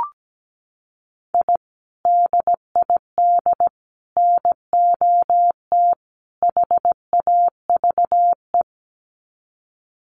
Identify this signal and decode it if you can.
Morse code, heard as a keyed tone, about 17 wpm, 700 Hz. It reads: I DID NOT HAVE
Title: Sets of 4 Words 17wpm